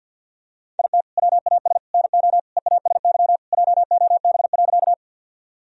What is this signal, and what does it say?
Morse code, heard as a keyed tone, about 50 wpm, 700 Hz. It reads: IT WAS NO EASY JOB.
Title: Sentences from Top 1000 Words - Review Part 3 - 50wpm